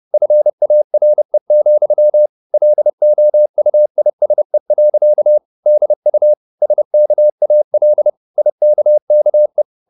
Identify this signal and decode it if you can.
Morse code, heard as a keyed tone, about 30 wpm, 600 Hz. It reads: FARE, LOUISE. DU SKAL IKKE
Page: Ferie 30 wpm